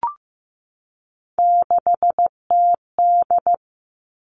Morse code, heard as a keyed tone, about 15 wpm, 700 Hz. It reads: 6TD
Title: Three Challenging Letter-Number Combinations 15wpm